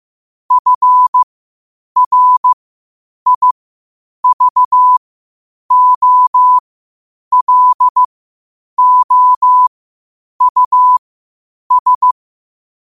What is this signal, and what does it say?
Morse code, heard as a keyed tone, about 15 wpm, spaced as for 8 wpm, 1000 Hz. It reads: FRIVOLOUS
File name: qotdfiles / 8wpm